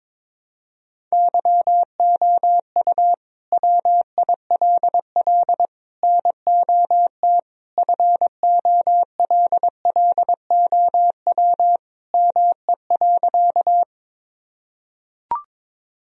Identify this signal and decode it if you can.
Morse code, heard as a keyed tone, about 22 wpm, 700 Hz. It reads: YOU WILL NOT FOLLOW ME.
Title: Sentences from Top 1000 Words - Review Part 1 - 22wpm